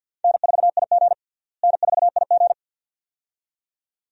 Morse code, heard as a keyed tone, about 49 wpm, 700 Hz. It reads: N4IG N4IG
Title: USA Call Signs - 4 Characters - Repeated 2x - 50wpm